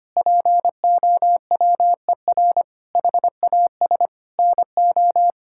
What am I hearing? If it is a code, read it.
Morse code, heard as a keyed tone, about 25 wpm, 700 Hz. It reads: POWER HAS NO
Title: Sentences from Top 1000 Words - Review Part 5 - 25wpm